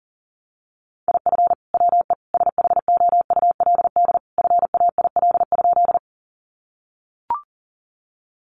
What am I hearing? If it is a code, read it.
Morse code, heard as a keyed tone, about 38 wpm, 700 Hz. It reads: IF WE SHOULD FAIL?
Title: Sentences from Top 1100 Words 40wpm